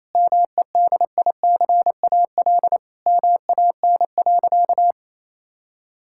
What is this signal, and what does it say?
Morse code, heard as a keyed tone, about 28 wpm, 700 Hz. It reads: MEDICAL MAN.
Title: Sentences from Top 1000 Words - Review Part 5 - 28wpm